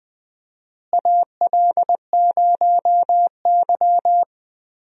Morse code, heard as a keyed tone, about 20 wpm, 700 Hz. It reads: AL0Y
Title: USA Call Signs - 4 Characters - Repeated 3x - 20wpm